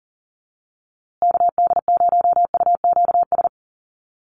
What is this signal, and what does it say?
Morse code, heard as a keyed tone, about 40 wpm, 700 Hz. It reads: KD0UQS
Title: USA Call Signs Repeated 3x 40wpm